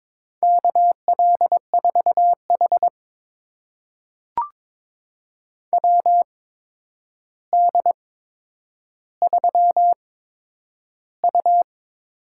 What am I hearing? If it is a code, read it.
Morse code, heard as a keyed tone, about 22 wpm, spaced as for 9 wpm, 700 Hz. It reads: KL4H WD3U
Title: USA Call Signs - 4 Characters - 8x Character Spacing - 22wpm